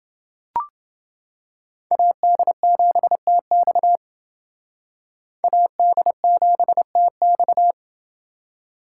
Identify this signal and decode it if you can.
Morse code, heard as a keyed tone, about 28 wpm, 700 Hz. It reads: AD7TX AD7TX